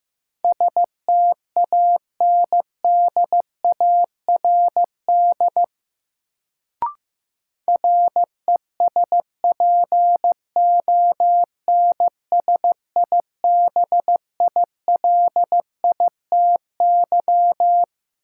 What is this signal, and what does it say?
Morse code, heard as a keyed tone, about 15 wpm, 700 Hz. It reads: STANDARD RESPONSIBILITY
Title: Sentences from Top 1200 Words - New Word Drill Alt - 15wpm